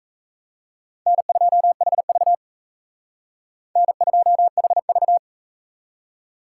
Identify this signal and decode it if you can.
Morse code, heard as a keyed tone, about 44 wpm, 700 Hz. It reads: N2HV N2HV